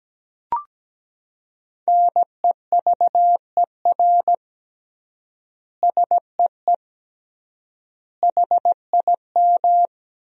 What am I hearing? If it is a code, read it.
Morse code, heard as a keyed tone, about 17 wpm, 700 Hz. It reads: NEVER SEE HIM
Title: 3x Word-Spacing - Sentences from Top 300 Words - 17wpm